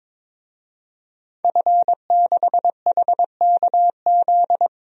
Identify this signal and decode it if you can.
Morse code, heard as a keyed tone, about 22 wpm, 700 Hz. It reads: F6HKZ
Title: DXCC-100 Call Signs - 5 Characters - Repeated 2x - 22wpm